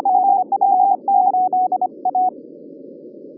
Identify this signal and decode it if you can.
Morse code, heard as a keyed tone, about 25 wpm, 740 Hz. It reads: DR8A